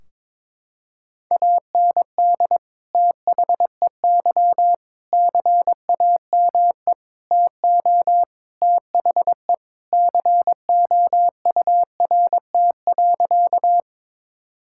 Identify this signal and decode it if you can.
Morse code, heard as a keyed tone, about 22 wpm, 700 Hz. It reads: AND THEY CAME TO THE COURT.
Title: Sentences from Top 1600 Words 22wpm